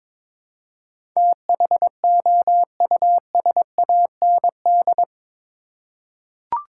Morse code, heard as a keyed tone, about 22 wpm, 700 Hz. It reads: THOUSAND
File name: practice / 22wpm / Eight Letter Words - 22wpm